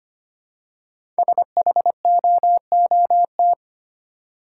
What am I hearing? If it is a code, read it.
Morse code, heard as a keyed tone, about 25 wpm, 700 Hz. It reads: SHOOT
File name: 5 Letter Words - First 3 Letters Given - 25wpm